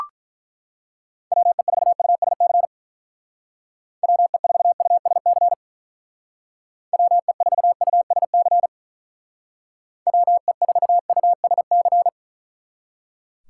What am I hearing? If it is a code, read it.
Morse code, heard as a keyed tone, about 43 wpm, 700 Hz. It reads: WE4USC WE4USC WE4USC WE4USC
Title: Speed Racing - USA 2x3 Call Signs 35wpm